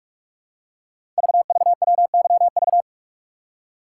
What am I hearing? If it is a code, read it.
Morse code, heard as a keyed tone, about 44 wpm, 700 Hz. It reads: VVWYV
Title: Code Groups - 5 Letters - 45wpm